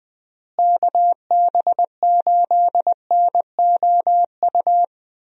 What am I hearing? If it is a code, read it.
Morse code, heard as a keyed tone, about 21 wpm, 700 Hz. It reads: KB8NOU
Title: USA Call Signs - 6 Characters - Repeated 2x - 20wpm